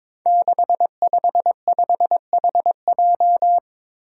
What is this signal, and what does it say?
Morse code, heard as a keyed tone, about 22 wpm, 700 Hz. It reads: 655HJ
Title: Five Challenging Letter-Number Combinations 22wpm